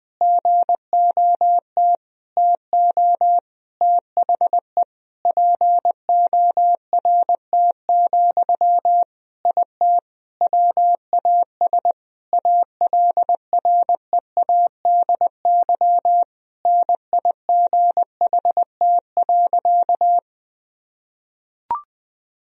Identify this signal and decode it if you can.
Morse code, heard as a keyed tone, about 19 wpm, 700 Hz. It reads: GOT TO THE PORT, IT WAS ALREADY NIGHT.
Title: Sentences from Top 1000 Words - Review Part 1 - 20wpm